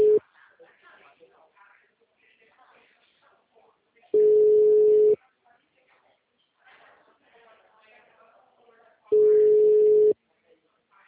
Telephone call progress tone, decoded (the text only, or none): ringback tone